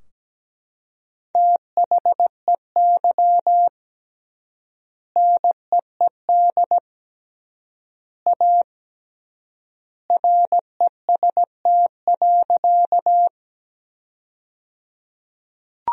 Morse code, heard as a keyed tone, about 17 wpm, 700 Hz. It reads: THEY NEED A REST.
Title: 3x Word-Spacing - Sentences from Top 500 Words - 17wpm